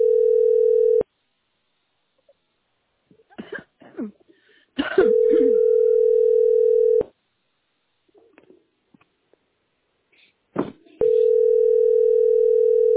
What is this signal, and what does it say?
Ringback tone (North America), 440 + 480 Hz.